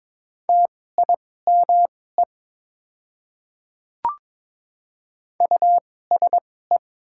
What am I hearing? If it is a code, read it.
Morse code, heard as a keyed tone, about 22 wpm, spaced as for 16 wpm, 700 Hz. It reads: TIME USE
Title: 2x Character Spacing - Top 200 Words - 22wpm